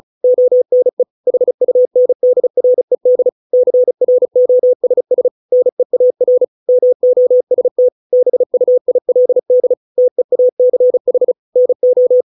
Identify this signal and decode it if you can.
Morse code, heard as a keyed tone, about 34 wpm, 500 Hz. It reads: ONE HUNDRED CROSS NEAR MOST BUILD TEACH NO